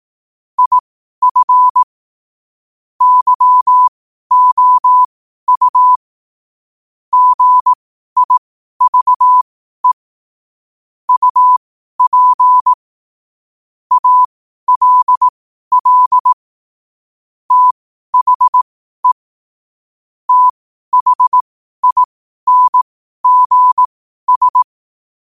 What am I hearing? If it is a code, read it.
Morse code, heard as a keyed tone, about 18 wpm, spaced as for 12 wpm, 1000 Hz. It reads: IF YOU GIVE UP ALL THE THINGS
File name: qotdfiles / 12wpm